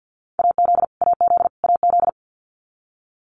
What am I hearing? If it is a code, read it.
Morse code, heard as a keyed tone, about 50 wpm, 700 Hz. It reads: AZ AZ AZ
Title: US States Repeated 3x 50wpm